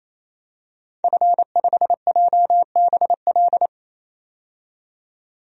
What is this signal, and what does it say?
Morse code, heard as a keyed tone, about 28 wpm, 700 Hz. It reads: F5JBL
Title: DXCC-100 Call Signs - 5 Characters - 28wpm